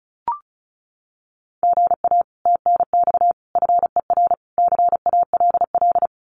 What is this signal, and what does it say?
Morse code, heard as a keyed tone, about 34 wpm, 700 Hz. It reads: GA TNX FER CALL
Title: Common QSO Phrases Encore 35wpm